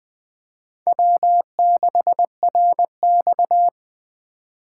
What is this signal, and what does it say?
Morse code, heard as a keyed tone, about 20 wpm, 700 Hz. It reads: W6RX
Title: USA Call Signs - 4 Characters - 4x Character Spacing - 20wpm